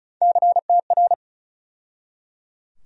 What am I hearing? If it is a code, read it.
Morse code, heard as a keyed tone, about 36 wpm, 700 Hz. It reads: CTR
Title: USA Call Sign Postfixes - 3 Letters 35wpm